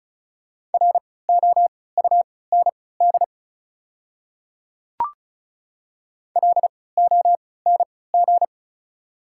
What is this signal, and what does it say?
Morse code, heard as a keyed tone, about 36 wpm, spaced as for 20 wpm, 700 Hz. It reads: ROUND LONG
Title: 3x Character Spacing - Top 500 Words - 35wpm